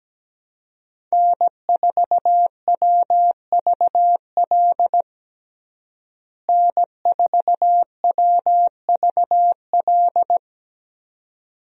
Morse code, heard as a keyed tone, about 17 wpm, 700 Hz. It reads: N4WVL N4WVL
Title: USA Call Signs - 5 Characters - Repeated 2x - 17wpm